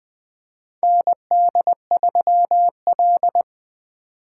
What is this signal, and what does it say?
Morse code, heard as a keyed tone, about 20 wpm, 700 Hz. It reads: ND3L
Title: K1USN SST QSO Practice - 20wpm